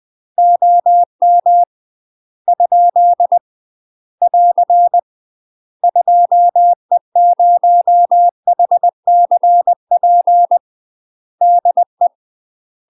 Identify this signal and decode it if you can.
Morse code, heard as a keyed tone, about 20 wpm, 700 Hz. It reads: OM ? <AR> 2E0HCP DE
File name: UK MASTER QSOs TRAINING / 20WPM / UK MASTER QSOs 0014